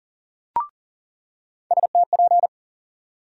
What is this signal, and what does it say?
Morse code, heard as a keyed tone, about 40 wpm, 700 Hz. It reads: STP
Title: USA Call Sign Postfixes - 3 Letters 40wpm